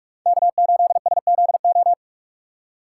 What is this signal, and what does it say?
Morse code, heard as a keyed tone, about 45 wpm, 700 Hz. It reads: K8SZO
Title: USA Call Signs - 5 Characters - 4x Character Spacing - 45wpm